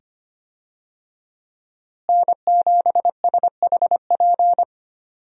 Morse code, heard as a keyed tone, about 25 wpm, 700 Hz. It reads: N7SHP
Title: USA Call Signs - 5 Characters - Repeated 2x - 25wpm